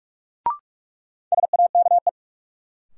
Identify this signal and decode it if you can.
Morse code, heard as a keyed tone, about 46 wpm, 700 Hz. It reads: SAKE
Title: Sets of 4 Words - New Word Drill - 45wpm